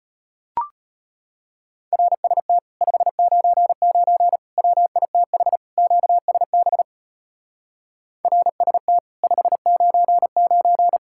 Morse code, heard as a keyed tone, about 33 wpm, 700 Hz. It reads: RST 599 WITH QSB RST 599